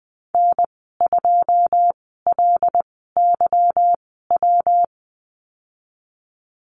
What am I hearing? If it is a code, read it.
Morse code, heard as a keyed tone, about 20 wpm, spaced as for 15 wpm, 700 Hz. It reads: N2LYW
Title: 179 Lesson 29g - Slash - Call Signs 20wpm